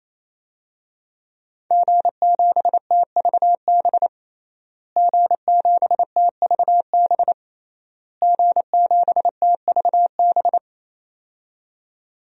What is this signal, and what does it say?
Morse code, heard as a keyed tone, about 28 wpm, 700 Hz. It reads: G7TVB G7TVB G7TVB
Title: DXCC-100 Call Signs - 5 Characters - Repeated 3x - 28wpm